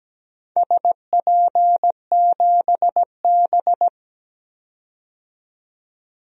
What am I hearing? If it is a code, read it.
Morse code, heard as a keyed tone, about 17 wpm, 700 Hz. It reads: SP7B
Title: DXCC-100 Call Signs - 4 Characters - 17wpm